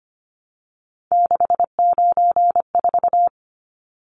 Morse code, heard as a keyed tone, about 25 wpm, 700 Hz. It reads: 694
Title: Numbers - 3-Digit 25wpm